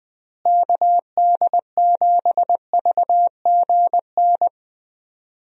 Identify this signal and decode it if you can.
Morse code, heard as a keyed tone, about 20 wpm, 700 Hz. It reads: KD7VGN
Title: USA Call Signs - 6 Characters - Repeated 2x - 20wpm